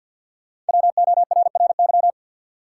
Morse code, heard as a keyed tone, about 53 wpm, 700 Hz. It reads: WORRY
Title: Sets of 5 Words - New Word Drill - 50wpm